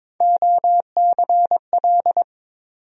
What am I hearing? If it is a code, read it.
Morse code, heard as a keyed tone, about 22 wpm, 700 Hz. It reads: OCL
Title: USA Call Sign Postfixes - 3 Letters 22wpm